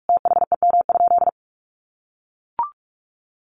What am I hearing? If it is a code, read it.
Morse code, heard as a keyed tone, about 44 wpm, 700 Hz. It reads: THEM?
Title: Sentences from Top 700 Words 45wpm